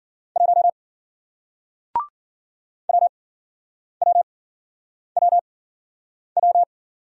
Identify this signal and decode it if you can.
Morse code, heard as a keyed tone, about 53 wpm, spaced as for 9 wpm, 700 Hz. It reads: 1 WWWW